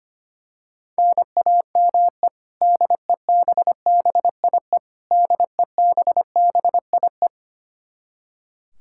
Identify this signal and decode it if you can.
Morse code, heard as a keyed tone, about 25 wpm, 700 Hz. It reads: NAME DEBBIE DEBBIE
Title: Practice QSO - Exchange 1 - Beginning - 25wpm